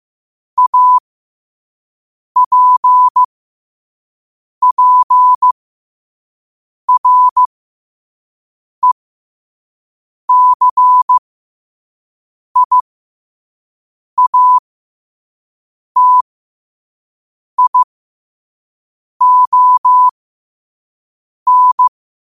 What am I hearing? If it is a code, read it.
Morse code, heard as a keyed tone, about 15 wpm, spaced as for 5 wpm, 1000 Hz. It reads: APPRECIATION